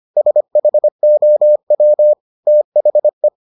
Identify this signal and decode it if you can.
Morse code, heard as a keyed tone, about 25 wpm, 600 Hz. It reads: SHOW THE